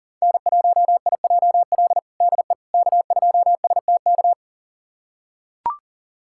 Morse code, heard as a keyed tone, about 40 wpm, 700 Hz. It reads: N1IJL DE K2STK
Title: USA Callsign DE Callsign - 5 Characters - 40wpm